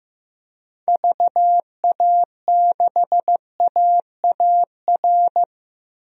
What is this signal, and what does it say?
Morse code, heard as a keyed tone, about 15 wpm, 700 Hz. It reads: VA6AAR
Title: DXCC-100 Call Signs - 6 Characters - 15wpm